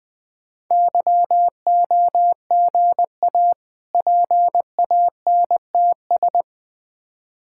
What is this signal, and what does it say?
Morse code, heard as a keyed tone, about 20 wpm, 700 Hz. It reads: YOGA PANTS
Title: Difficult Words with Hints - 20wpm